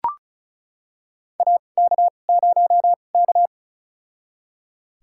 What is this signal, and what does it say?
Morse code, heard as a keyed tone, about 36 wpm, spaced as for 25 wpm, 700 Hz. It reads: AK0K
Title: USA Call Signs - 4 Characters - 2x Character Spacing - 35wpm